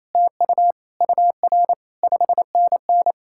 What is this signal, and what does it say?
Morse code, heard as a keyed tone, about 38 wpm, 700 Hz. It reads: TU UR 5NN